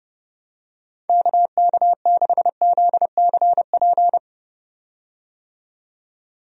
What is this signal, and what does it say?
Morse code, heard as a keyed tone, about 31 wpm, 700 Hz. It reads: KK6ZCP
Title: USA Call Signs - 6 Characters - 30wpm